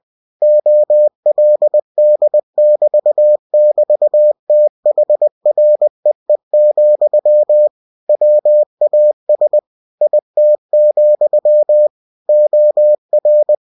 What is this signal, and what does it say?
Morse code, heard as a keyed tone, about 20 wpm, 600 Hz. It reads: OLD<BT><BT>THREE, WAS IT, OR
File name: cw-books / winnie-the-pooh / winnie-20wpm-page-0090